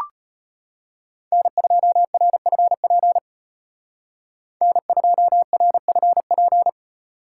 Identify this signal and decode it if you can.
Morse code, heard as a keyed tone, about 32 wpm, 700 Hz. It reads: N2RFP N2RFP